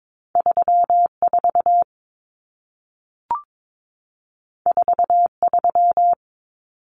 Morse code, heard as a keyed tone, about 22 wpm, 700 Hz. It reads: 34 43
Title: Common Cut Numbers - 2-Digit 22wpm